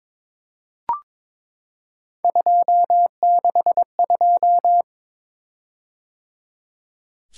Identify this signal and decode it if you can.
Morse code, heard as a keyed tone, about 22 wpm, 700 Hz. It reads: 262
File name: practice / 22wpm / Three Letter-Number Combinations - 22wpm